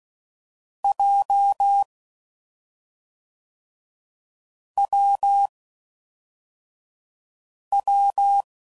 Morse code, heard as a keyed tone, about 16 wpm, 800 Hz. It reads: J W W
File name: Combinations / Difficult Combos / AWJ1